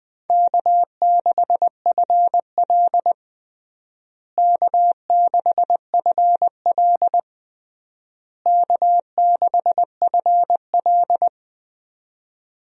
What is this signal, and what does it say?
Morse code, heard as a keyed tone, about 21 wpm, 700 Hz. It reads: K6FL K6FL K6FL
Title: USA Call Signs - 4 Characters - Repeated 3x - 20wpm